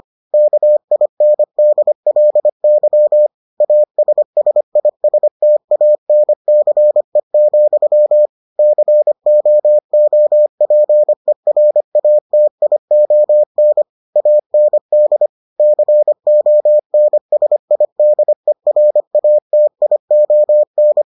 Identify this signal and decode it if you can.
Morse code, heard as a keyed tone, about 25 wpm, 600 Hz. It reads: KINDLY ASSISTANCE, COOPERATION AND CONSIDERATION